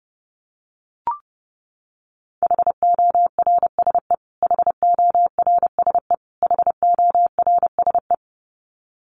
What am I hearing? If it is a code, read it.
Morse code, heard as a keyed tone, about 30 wpm, 700 Hz. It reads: HORSE HORSE HORSE
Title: Top 500 Words Repeated 30wpm